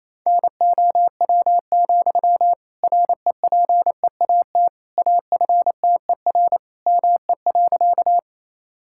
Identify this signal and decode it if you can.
Morse code, heard as a keyed tone, about 28 wpm, 700 Hz. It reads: NOW, REPEAT AFTER ME.
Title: Sentences from Top 1600 Words 28wpm